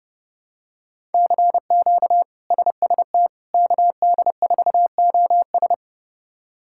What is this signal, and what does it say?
Morse code, heard as a keyed tone, about 29 wpm, 700 Hz. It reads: CQ SST KD4OS
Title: K1USN SST QSO Practice - 30wpm